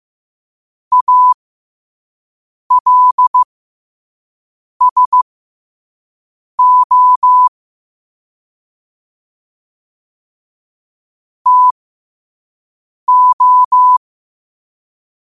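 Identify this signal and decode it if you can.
Morse code, heard as a keyed tone, about 15 wpm, spaced as for 5 wpm, 1000 Hz. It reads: ALSO TO
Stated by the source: Morse code practice at 5 wpm.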